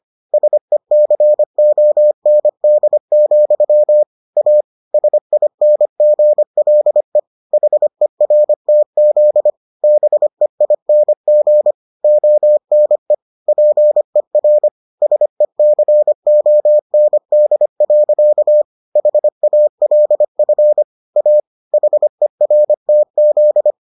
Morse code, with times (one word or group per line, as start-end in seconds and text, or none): 0.3-4.0 SECOND,
4.4-4.6 A
4.9-7.2 SINGLE
7.5-9.5 HERTZ
9.8-11.7 BEING
12.0-13.1 ONE
13.5-14.7 PER
15.0-18.6 SECOND.
19.0-20.8 HALF
21.2-21.4 A
21.7-23.7 HERTZ